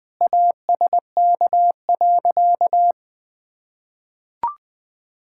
Morse code, heard as a keyed tone, about 20 wpm, 700 Hz. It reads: ASK.
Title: Sentences with 8-Word Phrases - Spoken Beginning - 20wpm